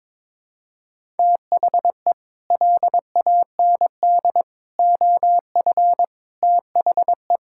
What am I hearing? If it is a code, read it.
Morse code, heard as a keyed tone, about 22 wpm, 700 Hz. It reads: THE LAND OF THE
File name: Sets of 9 Words - 22wpm